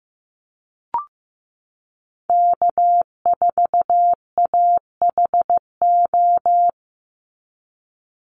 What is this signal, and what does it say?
Morse code, heard as a keyed tone, about 15 wpm, 700 Hz. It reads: K4AHO
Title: Call Signs Repeated 3x - 15wpm